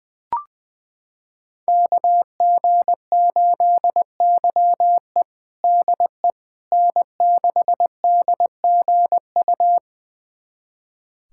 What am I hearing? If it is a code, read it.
Morse code, heard as a keyed tone, about 20 wpm, 700 Hz. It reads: KG8YE DE N6DGU
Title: USA Callsign DE Callsign - 5 Characters - 20wpm